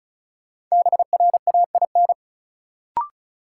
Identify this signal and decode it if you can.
Morse code, heard as a keyed tone, about 36 wpm, 700 Hz. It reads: BRAIN
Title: Sentences from Top 1000 Words - Review Part 6 - New Word Drill - 35wpm